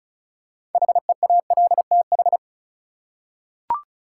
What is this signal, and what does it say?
Morse code, heard as a keyed tone, about 35 wpm, 700 Hz. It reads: HEALTH
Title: Six Letter Words - 35wpm